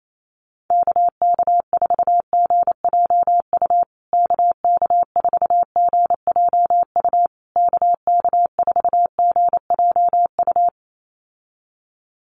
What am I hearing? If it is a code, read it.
Morse code, heard as a keyed tone, about 28 wpm, 700 Hz. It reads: KK4GJU KK4GJU KK4GJU
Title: USA Call Signs Repeated 3x 28wpm